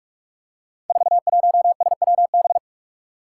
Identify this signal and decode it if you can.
Morse code, heard as a keyed tone, about 45 wpm, 700 Hz. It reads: 41SWB
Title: Five Challenging Letter-Number Combinations 45wpm